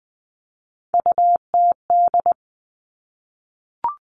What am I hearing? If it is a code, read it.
Morse code, heard as a keyed tone, about 20 wpm, 700 Hz. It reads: UTD
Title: Three Challenging Letter-Number Combinations 20wpm